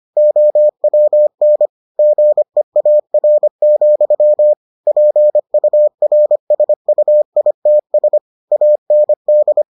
Morse code, heard as a keyed tone, about 25 wpm, 600 Hz. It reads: OWN GEAR, PURSUITS AND